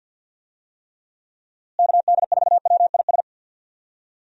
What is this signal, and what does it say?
Morse code, heard as a keyed tone, about 47 wpm, 700 Hz. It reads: KD4WIS
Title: USA Call Signs - 6 Characters - Repeated 2x - 50wpm